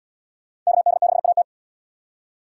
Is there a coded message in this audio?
Yes: Morse code from 0.7 to 1.4 s, heard as a keyed tone, about 77 wpm, 700 Hz. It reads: KD6AT